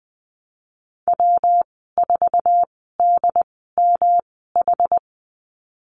Morse code, heard as a keyed tone, about 20 wpm, spaced as for 15 wpm, 700 Hz. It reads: W4DMH
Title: 140 Lesson 23g - Letter f - Call Signs 20wpm